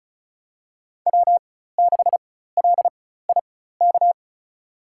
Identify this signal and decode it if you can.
Morse code, heard as a keyed tone, about 36 wpm, spaced as for 16 wpm, 700 Hz. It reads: W6LIK
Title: USA Call Signs - 5 Characters - 4x Character Spacing - 35wpm